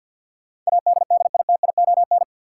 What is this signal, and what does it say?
Morse code, heard as a keyed tone, about 53 wpm, 700 Hz. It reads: ADDITION
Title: Sentences from Top 1000 Words - Review Part 6 - New Word Drill - 50wpm